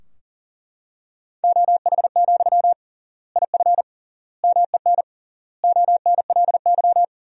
Morse code, heard as a keyed tone, about 37 wpm, 700 Hz. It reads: OH, IF MEN ONLY
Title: 3x Word-Spacing - Sentences from Top 500 Words - 40wpm